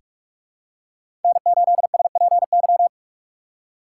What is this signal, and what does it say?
Morse code, heard as a keyed tone, about 46 wpm, 700 Hz. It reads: N8SPY
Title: USA Call Signs - 5 Characters - 4x Character Spacing - 45wpm